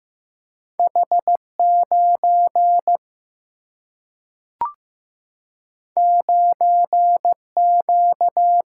Morse code, heard as a keyed tone, about 15 wpm, 700 Hz. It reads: H9 9Q
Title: Single-Letter Single-Number 15wpm